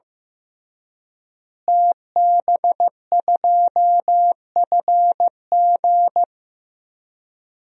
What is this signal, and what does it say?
Morse code, heard as a keyed tone, about 15 wpm, 700 Hz. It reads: TB2FG
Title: DXCC-100 Call Signs - 5 Characters - Repeated 3x - 15wpm